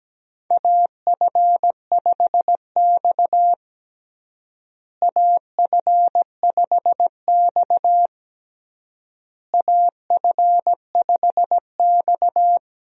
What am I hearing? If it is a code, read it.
Morse code, heard as a keyed tone, about 17 wpm, 700 Hz. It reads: AF5X AF5X AF5X
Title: USA Call Signs - 4 Characters - Repeated 3x - 17wpm